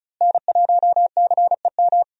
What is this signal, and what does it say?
Morse code, heard as a keyed tone, about 36 wpm, 700 Hz. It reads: N1CEM
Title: USA Call Signs - 5 Characters - Repeated 2x - 35wpm